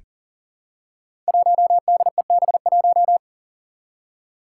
Morse code, heard as a keyed tone, about 40 wpm, 700 Hz. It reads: 1DEB1
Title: Five Challenging Letter-Number Combinations 40wpm